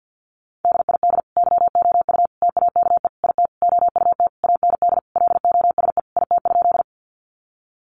Morse code, heard as a keyed tone, about 50 wpm, 700 Hz. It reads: DID YOU TAKE IT OUT AND LOSE IT?
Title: Sentences from Top 1100 Words 50wpm